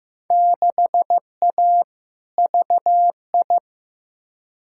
Morse code, heard as a keyed tone, about 15 wpm, 700 Hz. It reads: 6A VI
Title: Field Day Exchanges Encore - 15wpm